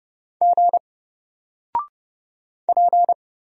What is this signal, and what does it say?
Morse code, heard as a keyed tone, about 31 wpm, 700 Hz. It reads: G P
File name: Single-Letter-Number-30wpm